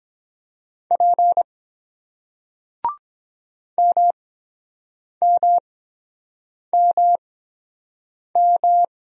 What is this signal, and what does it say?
Morse code, heard as a keyed tone, about 23 wpm, spaced as for 7 wpm, 700 Hz. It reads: P MMMM